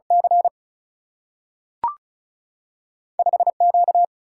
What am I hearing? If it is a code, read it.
Morse code, heard as a keyed tone, about 36 wpm, 700 Hz. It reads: C 5Q
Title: Single-Letter Single-Number 35wpm